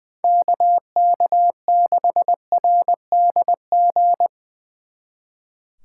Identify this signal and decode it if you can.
Morse code, heard as a keyed tone, about 24 wpm, 700 Hz. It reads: KK6RDG
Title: Speed Racing - USA 2x3 Call Signs 20wpm